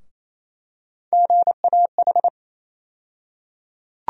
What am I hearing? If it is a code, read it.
Morse code, heard as a keyed tone, about 28 wpm, 700 Hz. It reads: GAH
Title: USA Call Sign Postfixes - 3 Letters 28wpm